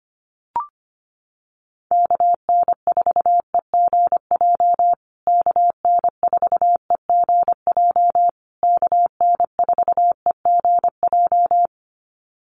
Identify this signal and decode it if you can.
Morse code, heard as a keyed tone, about 25 wpm, 700 Hz. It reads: KN4EGJ KN4EGJ KN4EGJ
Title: USA Call Signs Repeated 3x 25wpm